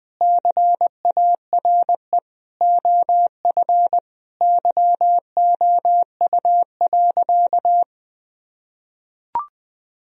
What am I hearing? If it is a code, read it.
Morse code, heard as a keyed tone, about 21 wpm, 700 Hz. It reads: CARE OF YOU.
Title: Sentences from Top 1000 Words - Review Part 1 - 20wpm